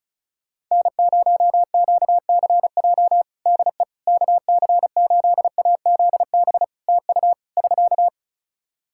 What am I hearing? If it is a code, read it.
Morse code, heard as a keyed tone, about 34 wpm, 700 Hz. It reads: N0QCJ DE KC8AZB TU <SK>
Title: Practice QSO - Exchange 4 - Ending - 35wpm